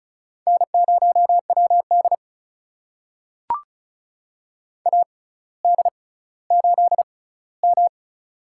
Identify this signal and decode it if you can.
Morse code, heard as a keyed tone, about 36 wpm, spaced as for 18 wpm, 700 Hz. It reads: N0WD AD8M
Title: USA Call Signs - 4 Characters - 6x Character Spacing - 35wpm